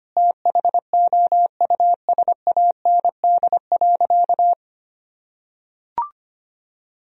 Morse code, heard as a keyed tone, about 25 wpm, 700 Hz. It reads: THOUSAND.
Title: Sentences from Top 1000 Words - Review Part 3 - 25wpm